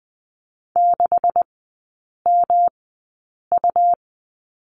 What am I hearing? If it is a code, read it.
Morse code, heard as a keyed tone, about 20 wpm, spaced as for 8 wpm, 700 Hz. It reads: 6MU
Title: 145 Lesson 24e - Letter y - Three Characters 20wpm